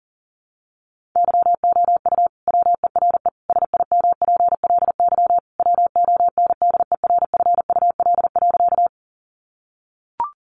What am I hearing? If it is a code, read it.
Morse code, heard as a keyed tone, about 37 wpm, 700 Hz. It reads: YOU WERE SIMPLY WONDERFUL.
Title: Sentences from Top 1100 Words 40wpm